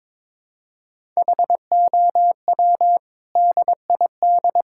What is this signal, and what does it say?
Morse code, heard as a keyed tone, about 22 wpm, 700 Hz. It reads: HOW DID
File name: practice / 22wpm / Sentences with 8-Word Phrases - 22wpm